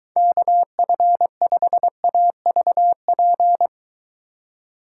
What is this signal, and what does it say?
Morse code, heard as a keyed tone, about 23 wpm, 700 Hz. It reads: KF5AVP